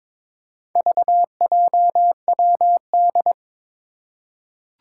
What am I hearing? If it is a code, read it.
Morse code, heard as a keyed tone, about 22 wpm, 700 Hz. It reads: VJWD
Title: Four Challenging Letter-Number Combinations 22wpm